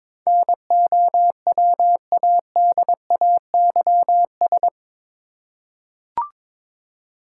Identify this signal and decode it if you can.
Morse code, heard as a keyed tone, about 22 wpm, 700 Hz. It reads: NOWADAYS
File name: practice / 22wpm / Eight Letter Words - 22wpm